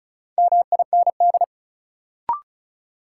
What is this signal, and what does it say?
Morse code, heard as a keyed tone, about 36 wpm, 700 Hz. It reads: MIND
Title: Sets of 5 Words - New Word Drill - 35wpm